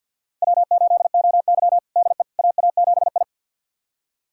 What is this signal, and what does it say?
Morse code, heard as a keyed tone, about 46 wpm, 700 Hz. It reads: W8OY DE AA7I
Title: USA Callsign DE Callsign - 4 Characters - 50wpm